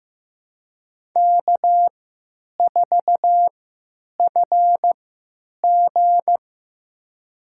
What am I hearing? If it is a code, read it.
Morse code, heard as a keyed tone, about 15 wpm, spaced as for 9 wpm, 700 Hz. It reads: K4FG
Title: USA Call Signs - 4 Characters - 3x Character Spacing - 15wpm